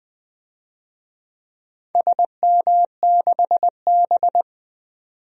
Morse code, heard as a keyed tone, about 20 wpm, 700 Hz. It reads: SM6B
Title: DXCC-100 Call Signs - 4 Characters - 20wpm